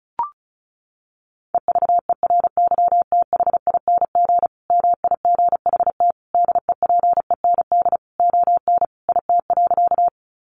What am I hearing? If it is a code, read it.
Morse code, heard as a keyed tone, about 34 wpm, 700 Hz. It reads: EVERYTHING MIGHT DEPEND ON IT.
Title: Sentences from Top 1300 Words 35wpm